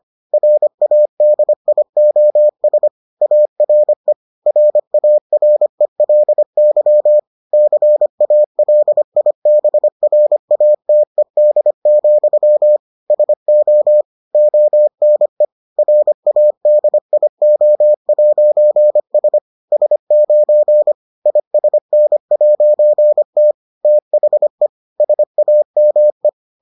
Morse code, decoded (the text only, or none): RADIOS ARE RARELY CALIBRATED, SO ONE RADIO'S S9 ISN'T THE SAME